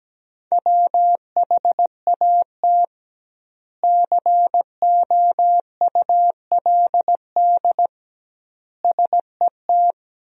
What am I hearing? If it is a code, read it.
Morse code, heard as a keyed tone, about 17 wpm, 700 Hz. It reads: WHAT COULD SET
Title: 2x Word-Spacing - Sentences from Top 200 Words - 17wpm